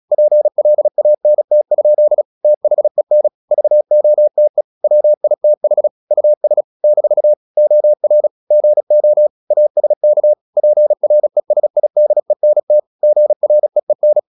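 Morse code, heard as a keyed tone, about 36 wpm, 600 Hz. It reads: PLANT? THEN VOTE WITH US - OR GO ASK PRESIDENT GREEN
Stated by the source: Morse practice at 36 wpm, text of Ken Follett's novel Never, from CW med Gnister.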